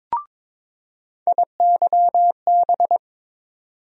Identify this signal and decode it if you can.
Morse code, heard as a keyed tone, about 22 wpm, 700 Hz. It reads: IYB
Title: USA Call Sign Postfixes - 3 Letters 22wpm